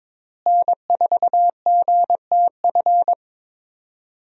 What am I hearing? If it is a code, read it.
Morse code, heard as a keyed tone, about 22 wpm, 700 Hz. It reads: N4GTF
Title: USA Call Signs - 5 Characters - Repeated 2x - 22wpm